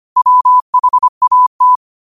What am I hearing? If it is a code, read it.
Morse code, heard as a keyed tone, about 25 wpm, 1000 Hz. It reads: WHAT
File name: qotdfiles / 25wpm